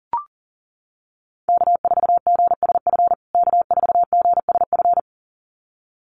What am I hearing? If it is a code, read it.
Morse code, heard as a keyed tone, about 40 wpm, 700 Hz. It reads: K4GSF K4GSF
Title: USA Call Signs Repeated 2x 40wpm